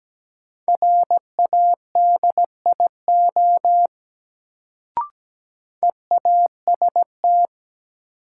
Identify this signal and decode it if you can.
Morse code, heard as a keyed tone, about 17 wpm, 700 Hz. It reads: RADIO EAST
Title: Sentences from Top 1000 Words - Review Part 3 - New Word Drill - 17wpm